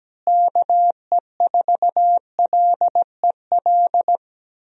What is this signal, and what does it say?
Morse code, heard as a keyed tone, about 17 wpm, 700 Hz. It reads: KE4LEL
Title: USA Call Signs - 6 Characters - Repeated 3x - 17wpm